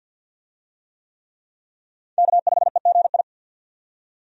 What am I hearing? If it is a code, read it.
Morse code, heard as a keyed tone, about 51 wpm, 700 Hz. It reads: K5EGI
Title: USA Call Signs - 5 Characters - 50wpm